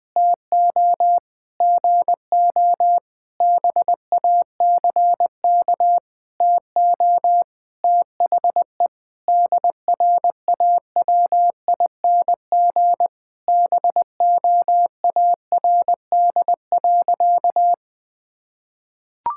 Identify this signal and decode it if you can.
Morse code, heard as a keyed tone, about 20 wpm, 700 Hz. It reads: TO GO BACK TO THE DRAWING BOARD.
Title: Sentences with 7-Word Phrases - 20wpm